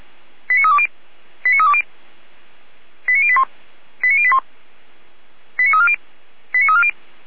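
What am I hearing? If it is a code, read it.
Five-tone selective call (ZVEI), heard as quick runs of five tones, tones of about 70 ms. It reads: 89320 89320 89081 89081 89340 89340
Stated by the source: ZVEI five-tone call 89315 received on an SDR, logged by feuerwehr-orlamuende.org fire brigade.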